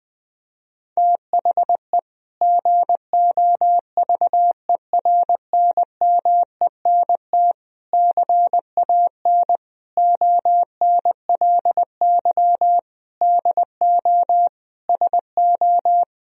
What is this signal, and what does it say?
Morse code, heard as a keyed tone, about 20 wpm, 700 Hz. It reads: THE GOVERNMENT CAN ONLY DO SO
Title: Sentences from Top 1000 Words - Review Part 3 - 20wpm